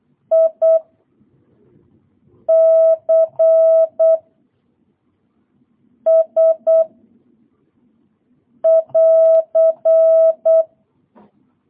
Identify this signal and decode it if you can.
Morse code, heard as a keyed tone, about 8 wpm, 660 Hz. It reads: I C S <AR>